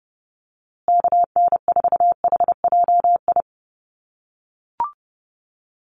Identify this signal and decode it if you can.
Morse code, heard as a keyed tone, about 30 wpm, 700 Hz. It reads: KN4HJI
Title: USA Call Signs Repeated 2x 30wpm